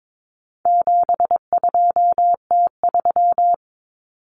Morse code, heard as a keyed tone, about 22 wpm, 700 Hz. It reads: 72T3
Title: Common Cut Numbers - 4-Digit 22wpm